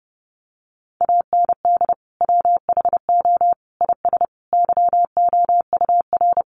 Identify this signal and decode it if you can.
Morse code, heard as a keyed tone, about 30 wpm, 700 Hz. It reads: AND WHO IS YOUR
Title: Sentences from Top 1300 Words 30wpm